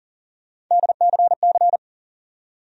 Morse code, heard as a keyed tone, about 40 wpm, 700 Hz. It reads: DCC
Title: USA Call Sign Postfixes - 3 Letters 40wpm